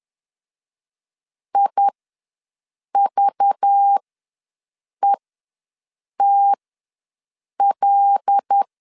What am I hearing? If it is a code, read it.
Morse code, heard as a keyed tone, about 10 wpm, spaced as for 6 wpm, 800 Hz. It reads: IVETL
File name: LEZIONI / 5 LBVC / 5 LBV ws 10wpm 3645